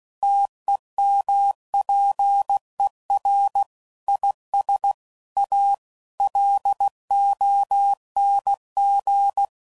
Morse code, heard as a keyed tone, about 16 wpm, 800 Hz. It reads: TEMPER IS A LONG